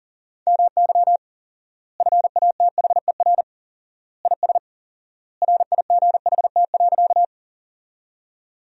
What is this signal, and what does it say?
Morse code, heard as a keyed tone, about 40 wpm, 700 Hz. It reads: MY FATHER IS RIGHT.
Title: 4x Word-Spacing - Sentences from Top 300 Words - 40wpm